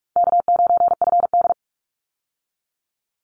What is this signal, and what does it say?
Morse code, heard as a keyed tone, about 44 wpm, 700 Hz. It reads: K9FD
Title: Call Signs Repeated 2x - 45wpm